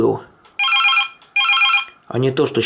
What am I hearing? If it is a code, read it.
Speech. There is no code.